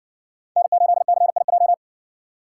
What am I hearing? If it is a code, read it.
Morse code, heard as a keyed tone, about 57 wpm, 700 Hz. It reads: N8CIJ